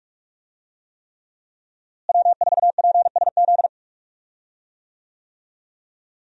Morse code, heard as a keyed tone, about 46 wpm, 700 Hz. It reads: W4PSZ
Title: USA Call Signs - 5 Characters - Repeated 3x - 45wpm